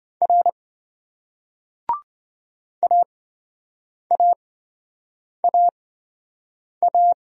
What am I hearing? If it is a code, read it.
Morse code, heard as a keyed tone, about 26 wpm, spaced as for 7 wpm, 700 Hz. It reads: R AAAA